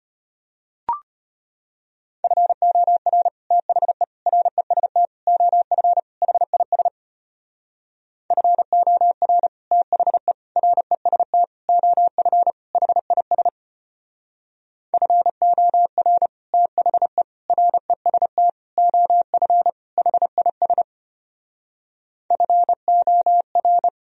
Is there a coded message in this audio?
Yes: Morse code from 2.2 to 23.9 s, heard as a keyed tone, about 32 wpm, 700 Hz. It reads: FOR THE REST OF HIS FOR THE REST OF HIS FOR THE REST OF HIS FOR